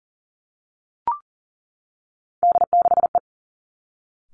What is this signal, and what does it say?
Morse code, heard as a keyed tone, about 40 wpm, 700 Hz. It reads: D6E
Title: Three Challenging Letter-Number Combinations 40wpm